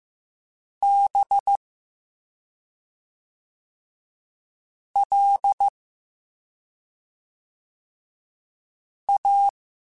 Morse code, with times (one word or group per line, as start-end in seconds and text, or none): 0.8-1.6 B
5.0-5.7 L
9.1-9.5 A